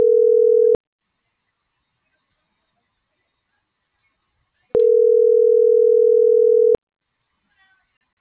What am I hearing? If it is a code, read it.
Ringback tone (North America), 440 + 480 Hz.